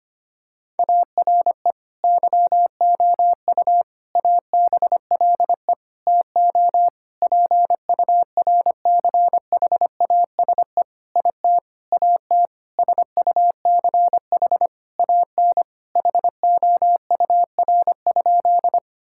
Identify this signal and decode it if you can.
Morse code, heard as a keyed tone, about 25 wpm, 700 Hz. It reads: ARE YOU ABLE TO PURCHASE IT AT SUCH AN HOUR?
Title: Sentences from Top 1000 Words - Review Part 5 - 25wpm